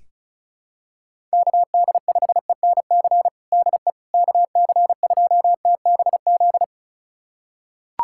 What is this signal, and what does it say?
Morse code, heard as a keyed tone, about 34 wpm, 700 Hz. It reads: KD5ENC DE KC2TBZ
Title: USA Callsign DE Callsign - 6 Characters - 35wpm